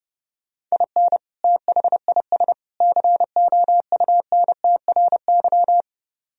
Morse code, heard as a keyed tone, about 33 wpm, 700 Hz. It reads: IN THIS COUNTRY